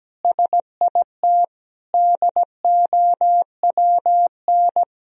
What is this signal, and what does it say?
Morse code, heard as a keyed tone, about 17 wpm, 700 Hz. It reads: SIT DOWN
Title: Sentences from Top 1000 Words - Review Part 1 - 17wpm